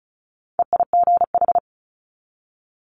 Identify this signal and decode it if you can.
Morse code, heard as a keyed tone, about 36 wpm, 700 Hz. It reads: EIGH
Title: Multi-letter Phonemes 35wpm